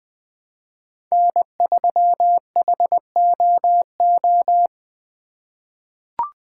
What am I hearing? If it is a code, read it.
Morse code, heard as a keyed tone, about 20 wpm, 700 Hz. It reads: N3HOO
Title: USA Call Signs - 5 Characters - 2x Character Spacing - 20wpm